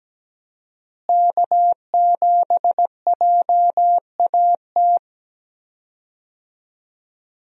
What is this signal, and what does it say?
Morse code, heard as a keyed tone, about 17 wpm, 700 Hz. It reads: K7JAT
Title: USA Call Signs - 5 Characters - Repeated 2x - 17wpm